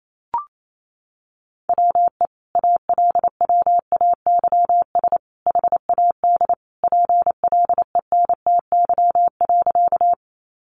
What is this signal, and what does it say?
Morse code, heard as a keyed tone, about 28 wpm, 700 Hz. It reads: WE ALWAYS HAD PLENTY.
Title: Sentences from Top 1300 Words 28wpm